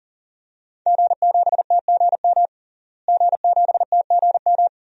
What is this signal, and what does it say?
Morse code, heard as a keyed tone, about 40 wpm, 700 Hz. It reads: G7TGM G7TGM
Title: DXCC-100 Call Signs - 5 Characters - Repeated 2x - 40wpm